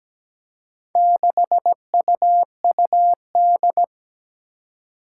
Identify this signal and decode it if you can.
Morse code, heard as a keyed tone, about 17 wpm, 700 Hz. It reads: 6UUD
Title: Four Challenging Letter-Number Combinations 17wpm